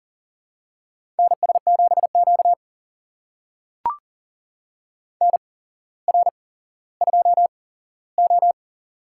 Morse code, heard as a keyed tone, about 42 wpm, spaced as for 17 wpm, 700 Hz. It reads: NS7Q NR2O
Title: USA Call Signs - 4 Characters - 8x Character Spacing - 40wpm